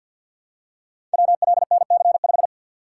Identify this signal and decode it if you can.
Morse code, heard as a keyed tone, about 51 wpm, 700 Hz. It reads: WLNC5
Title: Code Groups - 5 Letter:Numbers - 50wpm